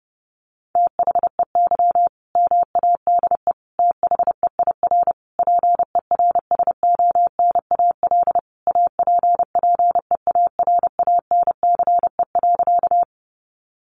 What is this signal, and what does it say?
Morse code, heard as a keyed tone, about 30 wpm, 700 Hz. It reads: THEY MADE THEIR PERSONAL APPEARANCE.
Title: Sentences from Top 1100 Words 30wpm